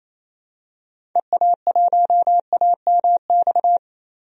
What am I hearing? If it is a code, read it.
Morse code, heard as a keyed tone, about 28 wpm, 700 Hz. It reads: EA1AMX
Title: DXCC-100 Call Signs - 6 Characters - 28wpm